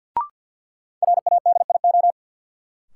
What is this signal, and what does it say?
Morse code, heard as a keyed tone, about 47 wpm, 700 Hz. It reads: RADIO
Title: Sentences from Top 1000 Words - Review Part 3 - New Word Drill - 50wpm